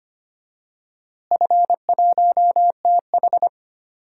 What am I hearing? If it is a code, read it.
Morse code, heard as a keyed tone, about 25 wpm, 700 Hz. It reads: F1TH
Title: DXCC-100 Call Signs - 4 Characters - Repeated 3x - 25wpm